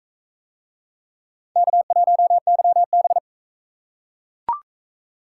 Morse code, heard as a keyed tone, about 42 wpm, 700 Hz. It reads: K1YB